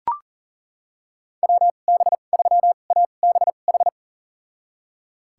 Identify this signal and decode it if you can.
Morse code, heard as a keyed tone, about 40 wpm, spaced as for 29 wpm, 700 Hz. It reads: WB3ABH
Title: USA Call Signs - 6 Characters - 2x Character Spacing - 40wpm